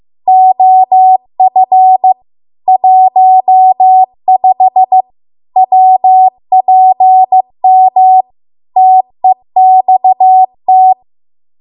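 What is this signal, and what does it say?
Morse code, heard as a keyed tone, about 15 wpm, 750 Hz. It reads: OF 15 WPM TEXT
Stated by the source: Morse code practice at 15 wpm from the national amateur radio association's code archive.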